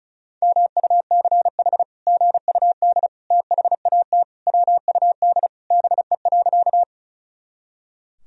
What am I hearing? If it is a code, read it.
Morse code, heard as a keyed tone, about 34 wpm, 700 Hz. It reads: MUCH GUD THAT WUD BE.
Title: Sentences from Top 200 Words and Abbreviations 35wpm